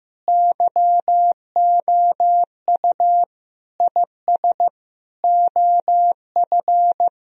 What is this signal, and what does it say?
Morse code, heard as a keyed tone, about 15 wpm, 700 Hz. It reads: YOU IS OF
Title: Sentences from Top 1000 Words - Review Part 3 - 15wpm